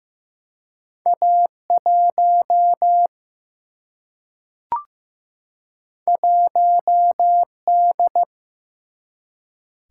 Morse code, heard as a keyed tone, about 15 wpm, 700 Hz. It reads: A1 1D
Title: Single-Letter Single-Number 15wpm